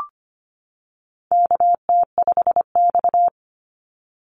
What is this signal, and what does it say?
Morse code, heard as a keyed tone, about 25 wpm, 700 Hz. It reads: KT5X
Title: NA SOTA Activator Callsigns 25wpm